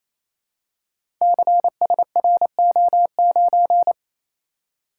Morse code, heard as a keyed tone, about 29 wpm, 700 Hz. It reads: CSRO9
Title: Code Groups - 5 Letter:Numbers - 28wpm